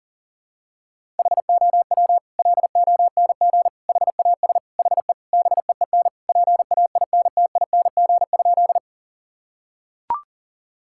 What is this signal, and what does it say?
Morse code, heard as a keyed tone, about 37 wpm, 700 Hz. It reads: HOW LONG HAS HE BEEN PAINTING?
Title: Sentences from Top 1000 Words - Review Part 6 - 40wpm